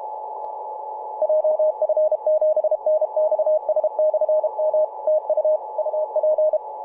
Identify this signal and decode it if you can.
Morse code, heard as a keyed tone, about 32 wpm, 620 Hz, with radio noise. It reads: JF7NXS/M TU UP
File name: vk9-g7vjr-24-903mhz-cw